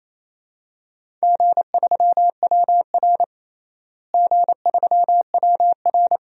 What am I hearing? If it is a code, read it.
Morse code, heard as a keyed tone, about 28 wpm, 700 Hz. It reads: G3WR G3WR
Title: DXCC-100 Call Signs - 4 Characters - Repeated 3x - 28wpm